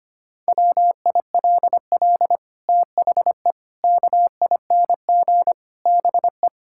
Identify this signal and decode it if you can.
Morse code, heard as a keyed tone, about 25 wpm, 700 Hz. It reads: WILL THE KING BE
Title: Sentences from Top 1000 Words - Review Part 6 - 25wpm